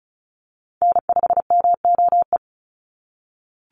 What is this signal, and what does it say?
Morse code, heard as a keyed tone, about 34 wpm, 700 Hz. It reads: N5MOE
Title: USA Call Signs 35wpm